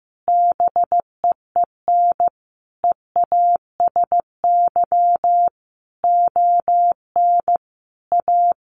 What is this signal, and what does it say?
Morse code, heard as a keyed tone, about 15 wpm, 700 Hz. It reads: BEEN EASY ON A
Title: Sentences from Top 800 Words 15wpm